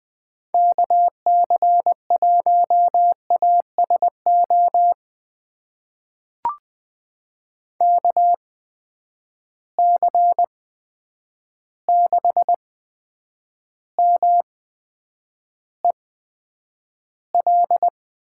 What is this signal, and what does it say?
Morse code, heard as a keyed tone, about 20 wpm, spaced as for 9 wpm, 700 Hz. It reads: KC1ASO KC6MEL
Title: USA Call Signs - 6 Characters - 8x Character Spacing - 20wpm